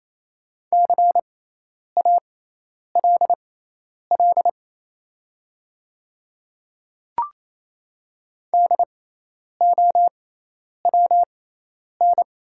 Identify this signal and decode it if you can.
Morse code, heard as a keyed tone, about 27 wpm, spaced as for 10 wpm, 700 Hz. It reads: CALL DOWN
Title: 6x Character Spacing - Top 100 Words - 28wpm